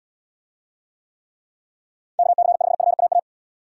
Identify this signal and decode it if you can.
Morse code, heard as a keyed tone, about 78 wpm, 700 Hz. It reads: KC5FAA